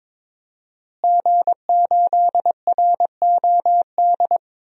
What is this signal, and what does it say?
Morse code, heard as a keyed tone, about 22 wpm, 700 Hz. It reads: G8ROD
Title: DXCC-100 Call Signs - 5 Characters - Repeated 3x - 22wpm